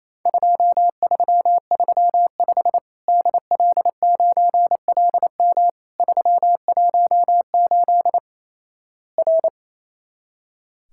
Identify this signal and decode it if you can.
Morse code, heard as a keyed tone, about 28 wpm, 700 Hz. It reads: 2335 DL9LM 318 R
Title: Worked All Europe DX Contest - QTC Exchange - 28wpm